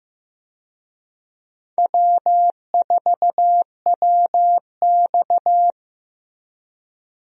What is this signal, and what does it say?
Morse code, heard as a keyed tone, about 15 wpm, 700 Hz. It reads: W4WX
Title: USA Call Signs - 4 Characters - Repeated 3x - 15wpm